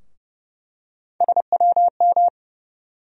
Morse code, heard as a keyed tone, about 31 wpm, 700 Hz. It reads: SWM
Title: USA Call Sign Postfixes - 3 Letters 30wpm